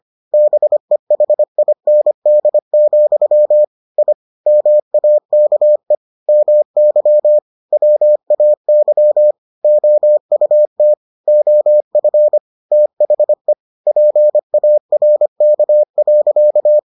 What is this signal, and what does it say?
Morse code, heard as a keyed tone, about 25 wpm, 600 Hz. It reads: BEHIND, I MAKE MY WAY OUT OF THE PARK.